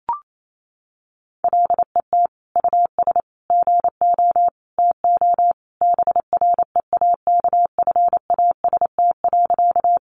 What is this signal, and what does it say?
Morse code, heard as a keyed tone, about 28 wpm, 700 Hz. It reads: LET US GO TO BREAKFAST.
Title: Sentences from Top 1500 Words 28wpm